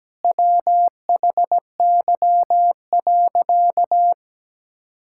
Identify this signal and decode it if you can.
Morse code, heard as a keyed tone, about 17 wpm, 700 Hz. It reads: WHY.
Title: Sentences from Top 1000 Words - Review Part 6 - 17wpm